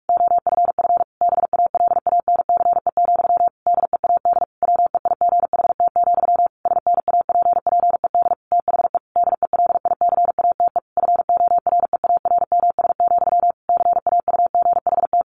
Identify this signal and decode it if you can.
Morse code, heard as a keyed tone, about 44 wpm, 700 Hz. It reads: OFF BALANCE, DEAD WEIGHT, SNAPPED THE DELICATE FOREARMS, CAUGHT
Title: A Sound Of Thunder 45wpm